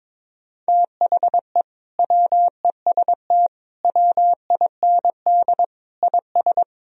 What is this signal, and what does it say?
Morse code, heard as a keyed tone, about 22 wpm, 700 Hz. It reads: THE WEST WIND IS
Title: Sentences from Top 1000 Words - Review Part 2 - 22wpm